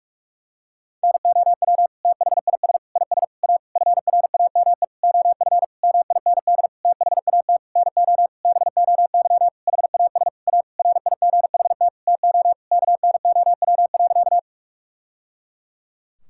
Morse code, heard as a keyed tone, about 46 wpm, 700 Hz. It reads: NOW THIS IS A FRAME OF MIND THAT NO BOY HAS A RIGHT TO KNOW.
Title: Sentences from Top 1000 Words - Review Part 6 - 45wpm